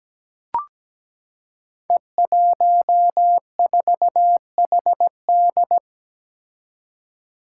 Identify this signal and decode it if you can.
Morse code, heard as a keyed tone, about 17 wpm, 700 Hz. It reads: E14HD
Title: Five Challenging Letter-Number Combinations 17wpm